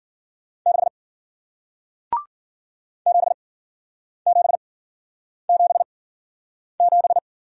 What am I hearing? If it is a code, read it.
Morse code, heard as a keyed tone, about 56 wpm, spaced as for 9 wpm, 700 Hz. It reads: 6 7777